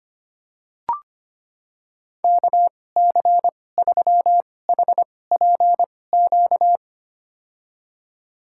Morse code, heard as a keyed tone, about 25 wpm, spaced as for 18 wpm, 700 Hz. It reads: KC3HPQ
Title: USA Call Signs - 6 Characters - 2x Character Spacing - 25wpm